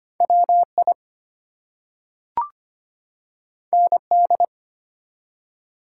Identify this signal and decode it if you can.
Morse code, heard as a keyed tone, about 25 wpm, 700 Hz. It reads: WI ND
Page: morse-code-ninja-practice-25wpm